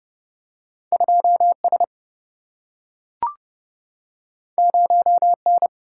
Morse code, heard as a keyed tone, about 29 wpm, 700 Hz. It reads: 2S 0N
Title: Single-Letter Single-Number 30wpm